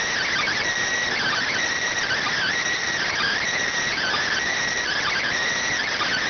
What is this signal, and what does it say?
Emergency Alert System (SAME) tones, heard as harsh buzzing bursts. It is heard from 1.1 to 5.3 s.